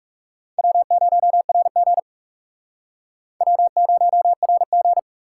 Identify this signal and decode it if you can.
Morse code, heard as a keyed tone, about 40 wpm, 700 Hz. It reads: W0RG W0RG